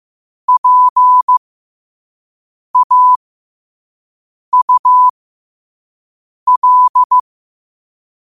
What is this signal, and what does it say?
Morse code, heard as a keyed tone, about 15 wpm, spaced as for 5 wpm, 1000 Hz. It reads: PAUL
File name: qotdfiles / 5wpm